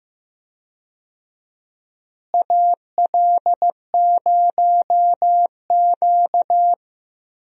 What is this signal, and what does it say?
Morse code, heard as a keyed tone, about 15 wpm, 700 Hz. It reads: AL0Q
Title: USA Call Signs - 4 Characters - Repeated 3x - 15wpm